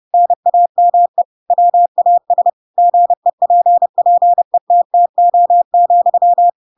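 Morse code, heard as a keyed tone, about 30 wpm, 700 Hz. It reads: NAME WAS GEPPETTO,
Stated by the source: Chapter 2 of The Adventures of Pinocchio read as a book in Morse code (CW) at 30 wpm, by ik1hge.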